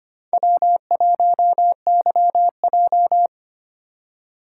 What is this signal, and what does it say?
Morse code, heard as a keyed tone, about 33 wpm, 700 Hz. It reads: W1YJ